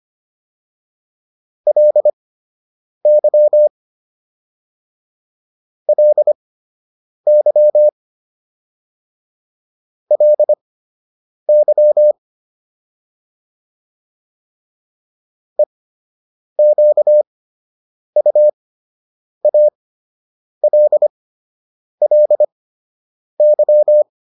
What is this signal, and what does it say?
Morse code, heard as a keyed tone, about 25 wpm, spaced as for 8 wpm, 600 Hz. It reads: LY LY LY EQUALLY